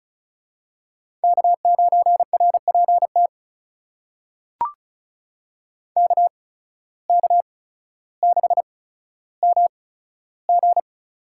Morse code, heard as a keyed tone, about 36 wpm, spaced as for 15 wpm, 700 Hz. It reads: K9RPT KK6MG
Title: USA Call Signs - 5 Characters - 8x Character Spacing - 35wpm